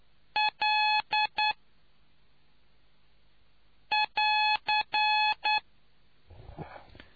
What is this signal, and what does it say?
Morse code, heard as a keyed tone, about 9 wpm, spaced as for 3 wpm, 850 Hz. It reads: L<AR>